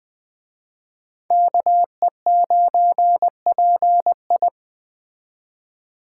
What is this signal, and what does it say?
Morse code, heard as a keyed tone, about 20 wpm, 700 Hz. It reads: KE9PI
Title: Speed Racing - USA 2x2 Call Signs 17wpm